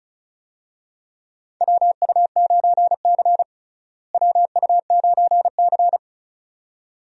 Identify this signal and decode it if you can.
Morse code, heard as a keyed tone, about 36 wpm, 700 Hz. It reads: WU9C WU9C
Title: USA Call Signs - 4 Characters - Repeated 2x - 35wpm